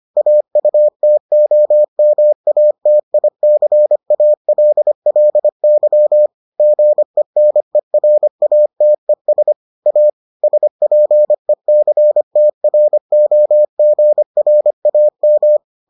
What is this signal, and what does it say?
Morse code, heard as a keyed tone, about 25 wpm, 600 Hz. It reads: AUTOMATICALLY GENERATES A SPECTROGRAM